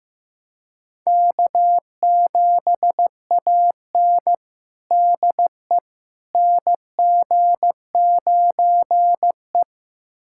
Morse code, heard as a keyed tone, about 15 wpm, 700 Hz. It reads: K7AN DE NG9E
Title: USA Callsign DE Callsign - 4 Characters - 15wpm